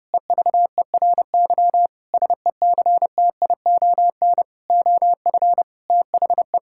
Morse code, heard as a keyed tone, about 30 wpm, 700 Hz. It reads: EVERY SECTION OF THE
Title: Sentences from Top 1000 Words - Review Part 5 - 30wpm